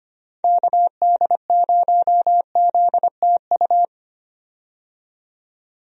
Morse code, heard as a keyed tone, about 25 wpm, 700 Hz. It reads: KD0ZTU
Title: USA Call Signs - 6 Characters - 25wpm